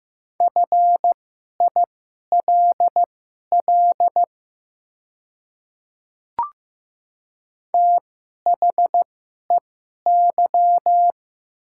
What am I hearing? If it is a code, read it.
Morse code, heard as a keyed tone, about 15 wpm, spaced as for 11 wpm, 700 Hz. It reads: FILL THEY
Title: 2x Character Spacing - Top 500 Words - 15wpm